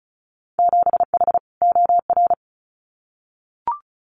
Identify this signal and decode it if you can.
Morse code, heard as a keyed tone, about 34 wpm, 700 Hz. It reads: 7H OR
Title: Winter Field Day Exchanges Encore 35wpm